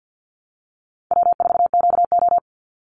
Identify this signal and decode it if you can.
Morse code, heard as a keyed tone, about 50 wpm, 700 Hz. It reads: W4QO
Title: Easy Call Signs Repeated 2x - 50wpm